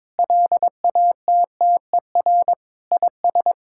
Morse code, heard as a keyed tone, about 22 wpm, 700 Hz. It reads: LATTER IS
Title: Sentences from Top 1600 Words 22wpm